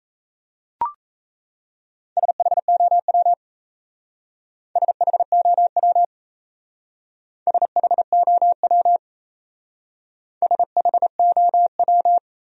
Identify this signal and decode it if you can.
Morse code, heard as a keyed tone, about 34 wpm, 700 Hz. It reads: SHOW SHOW SHOW SHOW
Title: Speed-Racing Top 500 Words 28wpm